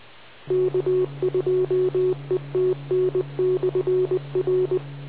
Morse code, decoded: K2ETN/R